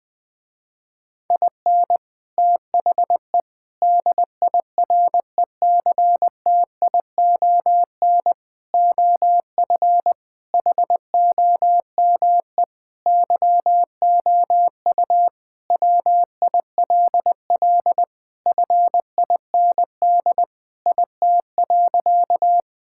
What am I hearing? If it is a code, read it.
Morse code, heard as a keyed tone, about 20 wpm, 700 Hz. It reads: IN THE DIRECTION OF HOME YOU WILL FIND IT.
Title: Sentences with 4-Word Phrases - Spoken Beginning - 20wpm